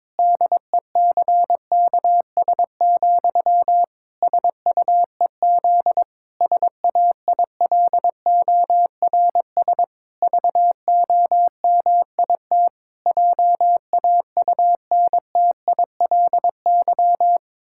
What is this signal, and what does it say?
Morse code, heard as a keyed tone, about 22 wpm, 700 Hz. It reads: DECKS, SUEZ SAILORS VOMIT JAUNTILY
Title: Pangram Challenge v2 - 22wpm